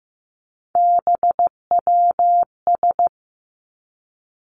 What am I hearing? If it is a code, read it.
Morse code, heard as a keyed tone, about 15 wpm, 700 Hz. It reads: BWS
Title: Three Challenging Letter-Number Combinations 15wpm